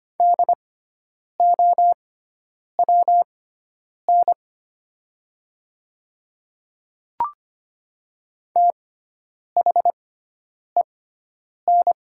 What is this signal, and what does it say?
Morse code, heard as a keyed tone, about 25 wpm, spaced as for 9 wpm, 700 Hz. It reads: DOWN THEN
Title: 6x Character Spacing - Top 300 Words - 25wpm